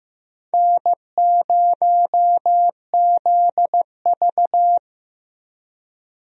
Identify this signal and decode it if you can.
Morse code, heard as a keyed tone, about 15 wpm, 700 Hz. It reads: N0ZV
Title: USA Call Signs - 4 Characters - Repeated 2x - 15wpm